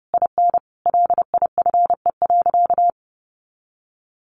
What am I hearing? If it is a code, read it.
Morse code, heard as a keyed tone, about 30 wpm, 700 Hz. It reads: IN LIFE.
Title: Sentences from Top 1200 Words 30wpm